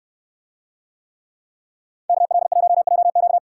Morse code, heard as a keyed tone, about 63 wpm, 690 Hz. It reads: KL1JQ